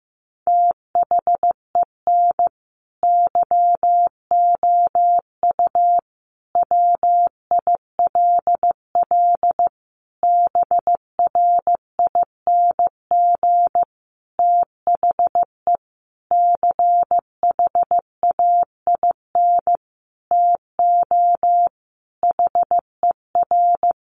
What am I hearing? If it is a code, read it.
Morse code, heard as a keyed tone, about 15 wpm, 700 Hz. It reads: THEN YOU WILL BRING THE CHAIN TO HER
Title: Sentences from Top 1500 Words 15wpm